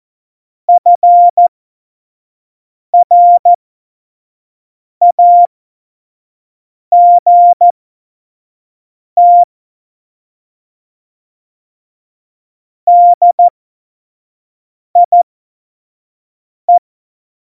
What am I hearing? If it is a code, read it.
Morse code, heard as a keyed tone, about 14 wpm, spaced as for 5 wpm, 700 Hz. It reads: FRAGT DIE
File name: Morsekurs / Witze / Witz 24.Juli 2018, 25 F70